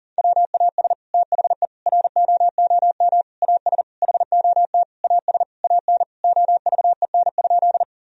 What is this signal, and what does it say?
Morse code, heard as a keyed tone, about 40 wpm, 700 Hz. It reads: WAS THE ROOM AS HOT AS AN OVEN?
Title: Sentences from Top 1000 Words - Review Part 4 - 40wpm